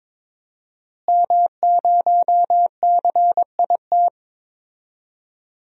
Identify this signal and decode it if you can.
Morse code, heard as a keyed tone, about 21 wpm, 700 Hz. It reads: M0CIT
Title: DXCC-100 Call Signs - 5 Characters - Repeated 3x - 22wpm